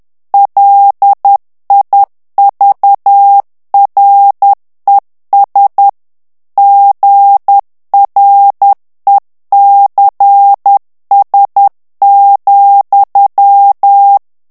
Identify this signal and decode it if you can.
Morse code, heard as a keyed tone, about 10 wpm, 800 Hz. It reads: LIVRES GRECS,
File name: morse mp3 / FRANCAIS TEXTE CLAIR 2 VITESSE 10 MOTS MINUTES